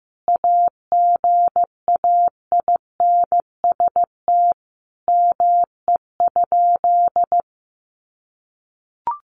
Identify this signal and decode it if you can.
Morse code, heard as a keyed tone, about 15 wpm, 700 Hz. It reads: AGAINST ME?
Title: Sentences from Top 600 Words 15wpm